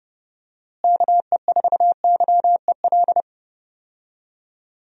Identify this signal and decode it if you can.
Morse code, heard as a keyed tone, about 32 wpm, 700 Hz. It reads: KE4YEL
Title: USA Call Signs - 6 Characters - Repeated 3x - 30wpm